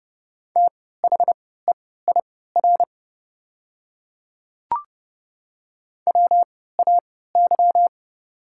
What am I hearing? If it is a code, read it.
Morse code, heard as a keyed tone, about 30 wpm, spaced as for 17 wpm, 690 Hz. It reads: THEIR WAY
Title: 3x Character Spacing - Top 100 Words - 30wpm